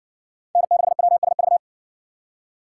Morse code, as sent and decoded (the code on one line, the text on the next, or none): -. -.... .-- ... ...-
N6WSV